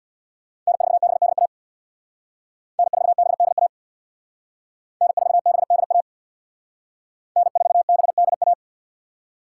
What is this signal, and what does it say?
Morse code, heard as a keyed tone, about 65 wpm, 700 Hz. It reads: N4BDA N4BDA N4BDA N4BDA